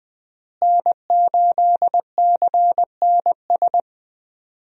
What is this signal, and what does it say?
Morse code, heard as a keyed tone, about 21 wpm, 700 Hz. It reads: N8CNS
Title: USA Call Signs - 5 Characters - Repeated 3x - 20wpm